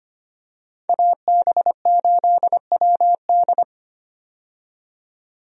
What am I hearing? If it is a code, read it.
Morse code, heard as a keyed tone, about 25 wpm, 700 Hz. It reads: AB8WD
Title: USA Call Signs - 5 Characters - 25wpm